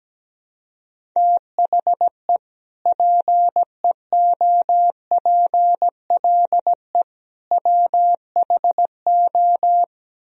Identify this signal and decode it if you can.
Morse code, heard as a keyed tone, about 17 wpm, 700 Hz. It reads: THE PEOPLE WHO
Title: Speed-Racing - Sets of 3 Words 17wpm